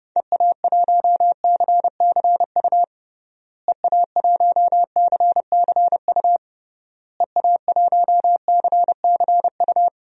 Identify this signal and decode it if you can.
Morse code, heard as a keyed tone, about 30 wpm, 700 Hz. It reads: EA1CCU EA1CCU EA1CCU
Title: DXCC-100 Call Signs - 6 Characters - Repeated 3x - 30wpm